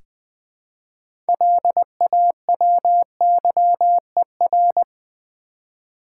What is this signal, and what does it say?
Morse code, heard as a keyed tone, about 20 wpm, 700 Hz. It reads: LAWYER
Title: Six Letter Words - 20wpm